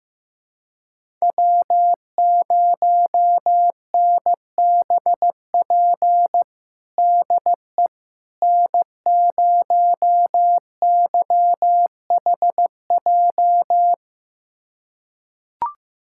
Morse code, heard as a keyed tone, about 15 wpm, 700 Hz. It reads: W0NBP DE N0YHJ
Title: USA Callsign DE Callsign - 5 Characters - 15wpm